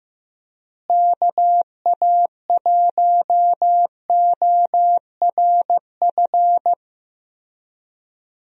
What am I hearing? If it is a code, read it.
Morse code, heard as a keyed tone, about 16 wpm, 700 Hz. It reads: KA1ORF
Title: USA Call Signs - 6 Characters - Repeated 3x - 15wpm